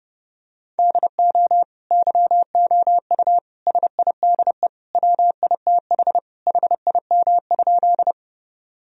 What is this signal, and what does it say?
Morse code, heard as a keyed tone, about 30 wpm, 700 Hz. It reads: DO YOU SIDE WITH HIM?
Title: Sentences from Top 1000 Words - Review Part 1 - 30wpm